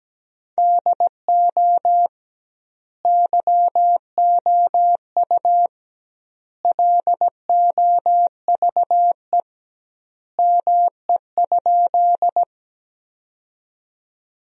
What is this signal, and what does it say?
Morse code, heard as a keyed tone, about 17 wpm, 700 Hz. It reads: DO YOU LOVE ME?
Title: 2x Word-Spacing - Sentences from Top 400 Words - 17wpm